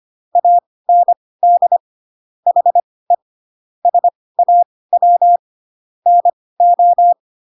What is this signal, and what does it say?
Morse code, heard as a keyed tone, about 25 wpm, spaced as for 18 wpm, 700 Hz. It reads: AND HE SAW NO
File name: Pinocchio / Pinocchio ch1 12 25 ik1hge